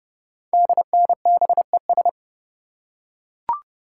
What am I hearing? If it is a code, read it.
Morse code, heard as a keyed tone, about 30 wpm, 690 Hz. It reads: DNBES
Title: Code Groups - 5 Letters - 30wpm